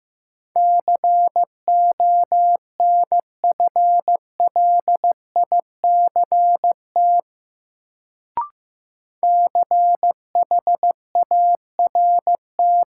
Morse code, heard as a keyed tone, about 15 wpm, 700 Hz. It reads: CONFLICT CHART
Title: Sentences from Top 1300 Words - New Word Drill Alt - 15wpm